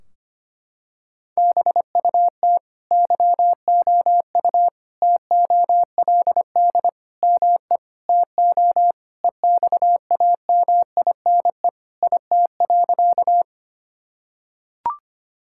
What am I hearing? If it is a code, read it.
Morse code, heard as a keyed tone, about 25 wpm, 700 Hz. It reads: BUT YOU TOLD ME TO EXAMINE IT.
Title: Sentences from Top 1600 Words 25wpm